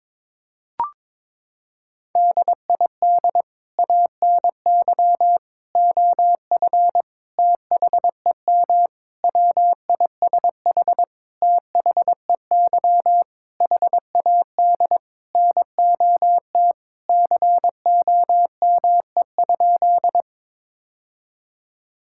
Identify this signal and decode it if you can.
Morse code, heard as a keyed tone, about 22 wpm, 700 Hz. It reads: DID ANY OF THEM WISH THEY HAD NOT COME?
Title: Sentences from Top 1600 Words 22wpm